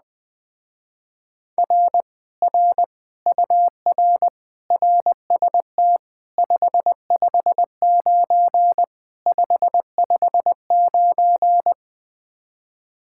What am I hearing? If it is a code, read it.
Morse code, heard as a keyed tone, about 21 wpm, 700 Hz. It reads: R R UR RST 559 559